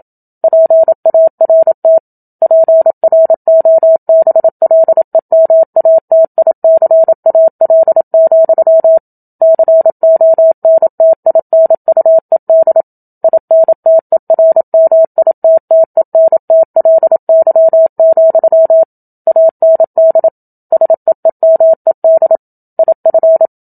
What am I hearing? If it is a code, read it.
Morse code, heard as a keyed tone, about 27 wpm, 650 Hz. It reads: PART PROBLEMATICAL, CONTINUED INTERMITTENTLY, AND SEEMED IF
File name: Book Two / 01-Chapter 1 - 27 WPM